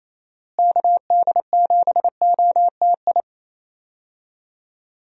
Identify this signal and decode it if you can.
Morse code, heard as a keyed tone, about 28 wpm, 700 Hz. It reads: KD7OTI
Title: USA Call Signs - 6 Characters - Repeated 2x - 28wpm